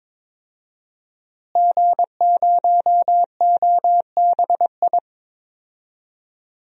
Morse code, heard as a keyed tone, about 22 wpm, 700 Hz. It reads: G0OBI
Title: DXCC-100 Call Signs - 5 Characters - Repeated 2x - 22wpm